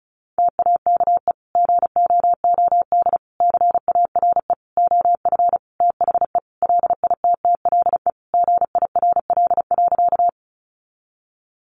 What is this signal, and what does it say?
Morse code, heard as a keyed tone, about 34 wpm, 700 Hz. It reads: TAKE GOOD CARE OF THE LITTLE GIRL.
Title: Sentences from Top 400 Words 35wpm